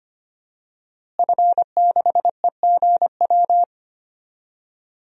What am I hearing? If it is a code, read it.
Morse code, heard as a keyed tone, about 25 wpm, 700 Hz. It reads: F6EGW
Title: DXCC-100 Call Signs - 5 Characters - Repeated 3x - 25wpm